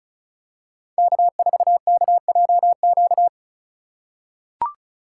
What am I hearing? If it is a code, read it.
Morse code, heard as a keyed tone, about 36 wpm, 700 Hz. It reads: K4KJQ
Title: USA Call Signs - 5 Characters - 2x Character Spacing - 35wpm